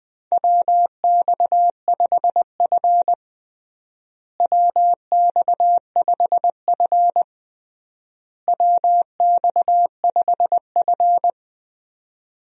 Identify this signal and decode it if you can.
Morse code, heard as a keyed tone, about 21 wpm, 700 Hz. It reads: WX5F WX5F WX5F
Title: USA Call Signs - 4 Characters - Repeated 3x - 20wpm